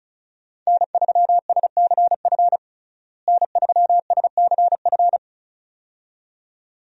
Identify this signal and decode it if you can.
Morse code, heard as a keyed tone, about 36 wpm, 700 Hz. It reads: N3SCF N3SCF
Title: USA Call Signs - 5 Characters - Repeated 2x - 35wpm